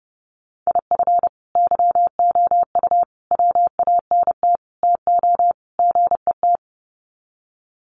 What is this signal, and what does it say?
Morse code, heard as a keyed tone, about 30 wpm, 700 Hz. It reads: IF YOU WANT TO GET
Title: Sets of 5 Words Encore 30wpm